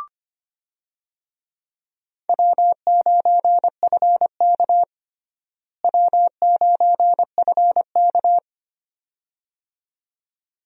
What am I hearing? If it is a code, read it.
Morse code, heard as a keyed tone, about 25 wpm, 700 Hz. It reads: W9FK W9FK
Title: USA Call Signs - 4 Characters - Repeated 2x - 25wpm